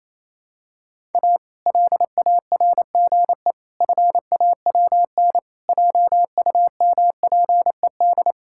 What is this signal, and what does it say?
Morse code, heard as a keyed tone, about 28 wpm, 700 Hz. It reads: A LARGE FAWN JUMPED
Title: Pangram Challenge v2 - 28wpm